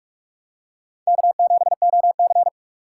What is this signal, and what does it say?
Morse code, heard as a keyed tone, about 46 wpm, 700 Hz. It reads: K7OC
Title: USA Call Signs - 4 Characters - 6x Character Spacing - 45wpm